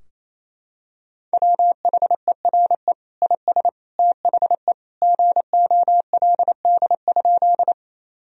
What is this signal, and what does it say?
Morse code, heard as a keyed tone, about 28 wpm, 700 Hz. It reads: WHERE IS THE GOLD?
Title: Sentences from Top 1000 Words - Review Part 3 - 28wpm